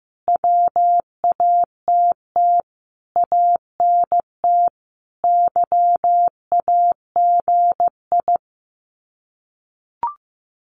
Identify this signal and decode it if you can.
Morse code, heard as a keyed tone, about 15 wpm, 700 Hz. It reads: WATT ANT YAGI
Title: Common QSO Phrases Encore 15wpm